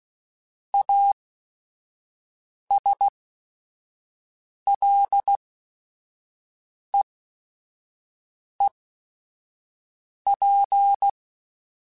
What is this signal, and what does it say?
Morse code, heard as a keyed tone, about 16 wpm, spaced as for 5 wpm, 800 Hz. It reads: ASLEEP
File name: Quotes / Quotes-mp3s / ChildAsleep @5